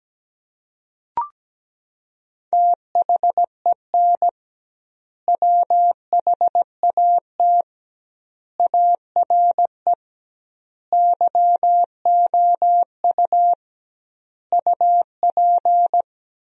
Morse code, heard as a keyed tone, about 17 wpm, 700 Hz. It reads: THEN WHAT ARE YOU UP
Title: 2x Word-Spacing - Sentences from Top 100 Words - 17wpm